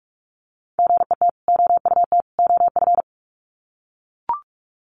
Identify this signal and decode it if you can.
Morse code, heard as a keyed tone, about 44 wpm, 700 Hz. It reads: GET OUT OF
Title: Sets of 3 Words 45wpm